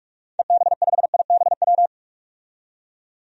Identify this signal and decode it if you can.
Morse code, heard as a keyed tone, about 46 wpm, 700 Hz. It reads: EB5IBW
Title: DXCC-100 Call Signs - 6 Characters - 45wpm